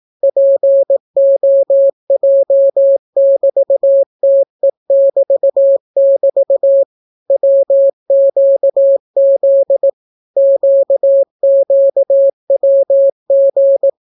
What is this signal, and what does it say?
Morse code, heard as a keyed tone, about 18 wpm, 550 Hz. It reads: POJ<BT>TE<BT><BT> WQZ QQWG